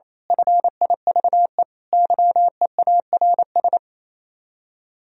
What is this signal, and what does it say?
Morse code, heard as a keyed tone, about 27 wpm, 700 Hz. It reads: FIVE YEARS
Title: Speed-Racing - Sets of 5 Words 28wpm